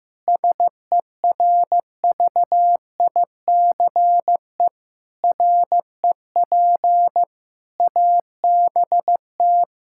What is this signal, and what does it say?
Morse code, heard as a keyed tone, about 15 wpm, 700 Hz. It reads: SERVICE REP ABT
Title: Practice QSO - Exchange 3 - More Info - 15wpm